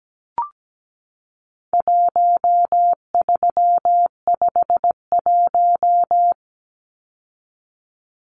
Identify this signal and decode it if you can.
Morse code, heard as a keyed tone, about 17 wpm, 700 Hz. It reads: 1351
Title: Numbers - 4-Digit 17wpm